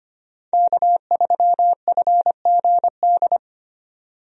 Morse code, heard as a keyed tone, about 25 wpm, 690 Hz. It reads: K3FGD
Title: USA Call Signs - 5 Characters - 25wpm